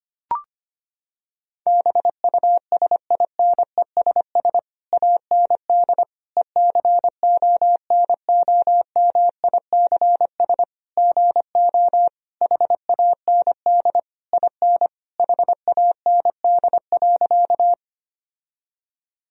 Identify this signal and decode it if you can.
Morse code, heard as a keyed tone, about 25 wpm, 700 Hz. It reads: BUSINESS AND ECONOMICS GO HAND IN HAND.
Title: Sentences from Top 1000 Words - Review Part 3 - 25wpm